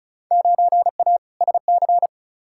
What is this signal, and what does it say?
Morse code, heard as a keyed tone, about 36 wpm, 700 Hz. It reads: 9A SC
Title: Field Day Exchanges Encore - 35wpm